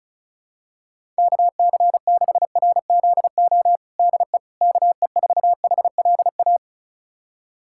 Morse code, heard as a keyed tone, about 34 wpm, 700 Hz. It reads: KC6RZO DE KE4HLA
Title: USA Callsign DE Callsign - 6 Characters - 35wpm